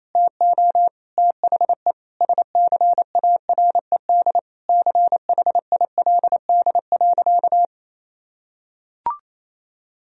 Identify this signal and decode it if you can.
Morse code, heard as a keyed tone, about 28 wpm, 700 Hz. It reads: TO THE SCARED CHILD.
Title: Sentences from Top 1000 Words - Review Part 4 - 28wpm